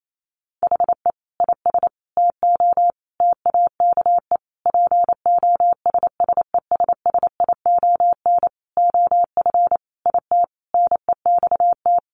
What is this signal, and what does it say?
Morse code, heard as a keyed tone, about 28 wpm, 700 Hz. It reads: HE IS TO TAKE POSSESSION OF IT NEXT
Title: Sentences from Top 1400 Words 28wpm